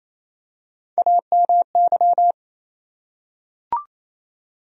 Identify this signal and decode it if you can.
Morse code, heard as a keyed tone, about 27 wpm, 700 Hz. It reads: AMY
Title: USA Common Names - Max 4 Letters - 28wpm